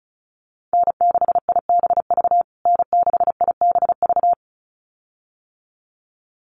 Morse code, heard as a keyed tone, about 34 wpm, 700 Hz. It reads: N6IBV N6IBV
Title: USA Call Signs Repeated 2x 35wpm